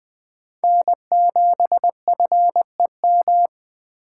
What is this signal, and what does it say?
Morse code, heard as a keyed tone, about 20 wpm, 700 Hz. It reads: N7FEM
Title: USA Call Signs - 5 Characters - 4x Character Spacing - 20wpm